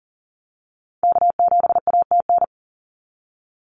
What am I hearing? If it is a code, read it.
Morse code, heard as a keyed tone, about 42 wpm, 700 Hz. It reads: K7ATN
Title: NA SOTA Activator Callsigns 40wpm